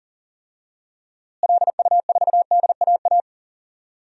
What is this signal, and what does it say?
Morse code, heard as a keyed tone, about 40 wpm, 700 Hz. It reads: LU4DAA
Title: DXCC-100 Call Signs - 6 Characters - Repeated 3x - 40wpm